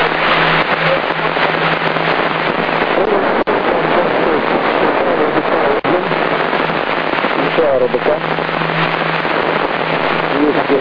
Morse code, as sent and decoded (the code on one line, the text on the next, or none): none